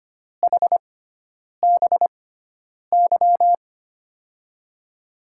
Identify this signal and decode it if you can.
Morse code, heard as a keyed tone, about 25 wpm, spaced as for 9 wpm, 700 Hz. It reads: HBY
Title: USA Call Signs - 6 Characters - 6x Character Spacing - 25wpm